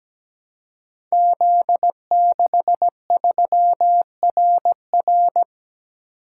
Morse code, heard as a keyed tone, about 17 wpm, 700 Hz. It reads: Z63RR
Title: Code Groups - 5 Letter:Numbers - 17wpm